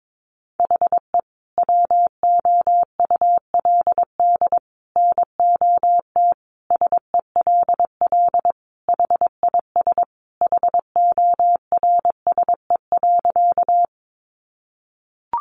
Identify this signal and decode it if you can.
Morse code, heard as a keyed tone, about 22 wpm, 700 Hz. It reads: HE WOULD NOT SELL HIS HORSE.
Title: Sentences from Top 1100 Words 22wpm